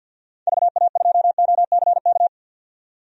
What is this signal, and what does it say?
Morse code, heard as a keyed tone, about 48 wpm, 700 Hz. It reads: VA2OCK
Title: DXCC-100 Call Signs - 6 Characters - 50wpm